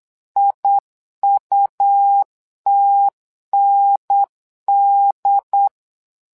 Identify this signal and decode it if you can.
Morse code, heard as a keyed tone, about 8 wpm, 800 Hz. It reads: IUTND
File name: LEZIONI / 2 UHDN / 2 UHDN 8wpm2232